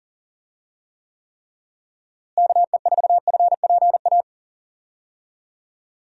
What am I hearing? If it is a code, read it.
Morse code, heard as a keyed tone, about 40 wpm, 700 Hz. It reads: KE4FPA
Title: USA Call Signs - 6 Characters - Repeated 3x - 40wpm